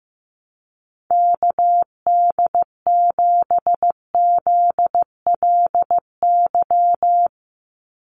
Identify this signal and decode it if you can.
Morse code, heard as a keyed tone, about 15 wpm, 700 Hz. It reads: KD7ZLY
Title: USA Call Signs 15wpm